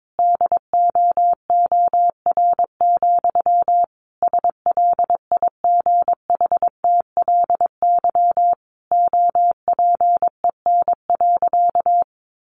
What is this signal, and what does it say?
Morse code, heard as a keyed tone, about 22 wpm, 700 Hz. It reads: DOOR, SLIGHTLY OPEN.
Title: Sentences from Top 1300 Words 22wpm